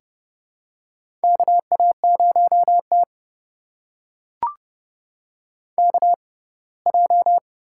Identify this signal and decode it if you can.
Morse code, heard as a keyed tone, about 30 wpm, spaced as for 20 wpm, 700 Hz. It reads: KA0T KJ
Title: USA Call Signs - 6x Character Spacing - 30wpm